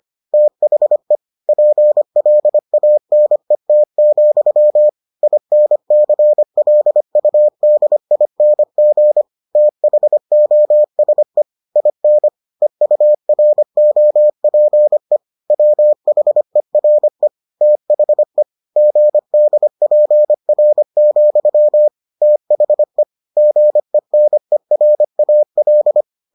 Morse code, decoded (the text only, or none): THE PLANET, INCLUDING THOSE IN EUROPE WHERE THE GDPR, THE GENERAL